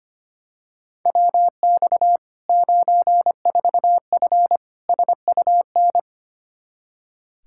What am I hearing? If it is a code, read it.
Morse code, heard as a keyed tone, about 25 wpm, 700 Hz. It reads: WX 94F SUN
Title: Practice QSO - Exchange 2 - Info - 25wpm